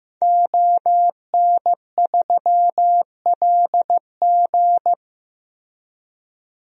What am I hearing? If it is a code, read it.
Morse code, heard as a keyed tone, about 15 wpm, 700 Hz. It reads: ON3LG
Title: DXCC-100 Call Signs - 5 Characters - Repeated 2x - 15wpm